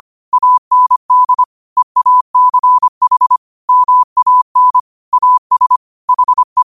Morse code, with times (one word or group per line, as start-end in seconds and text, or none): 0.3-1.4 AND
1.8-3.4 EACH
3.7-4.8 MAN
5.1-5.8 AS
6.1-6.6 HE